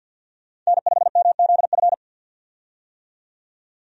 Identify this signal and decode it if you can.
Morse code, heard as a keyed tone, about 53 wpm, 700 Hz. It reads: N5MZF
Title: USA Call Signs - 5 Characters - 50wpm